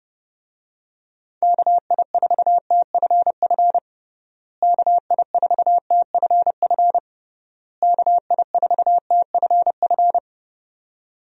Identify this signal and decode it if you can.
Morse code, heard as a keyed tone, about 29 wpm, 700 Hz. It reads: KI4TFF KI4TFF KI4TFF
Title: USA Call Signs - 6 Characters - Repeated 3x - 30wpm